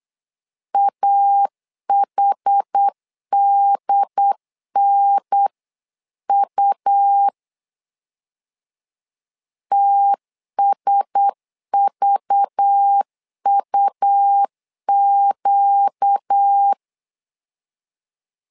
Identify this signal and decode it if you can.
Morse code, heard as a keyed tone, about 8 wpm, 800 Hz. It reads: AHDNU TSVUQ
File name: LEZIONI / 5 LBVC / 5 LBV 8wpm 2837